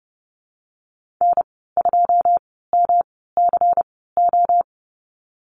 Morse code, heard as a keyed tone, about 30 wpm, spaced as for 17 wpm, 700 Hz. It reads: N2MCO
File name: Lesson 19g - Letter w - Call Signs 30wpm